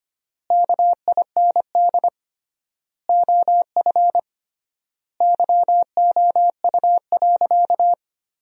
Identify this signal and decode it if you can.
Morse code, heard as a keyed tone, about 25 wpm, 700 Hz. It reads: KIND OF YOU.
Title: 3x Word-Spacing - Sentences from Top 200 Words - 25wpm